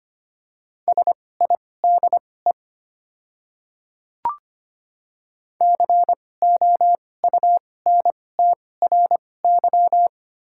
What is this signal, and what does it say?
Morse code, heard as a keyed tone, about 25 wpm, spaced as for 18 wpm, 700 Hz. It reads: SIDE COUNTRY
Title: 2x Character Spacing - Top 300 Words - 25wpm